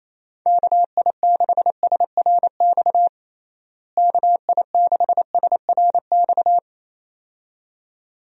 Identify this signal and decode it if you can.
Morse code, heard as a keyed tone, about 28 wpm, 700 Hz. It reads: KI6SRX KI6SRX
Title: USA Call Signs - 6 Characters - Repeated 2x - 28wpm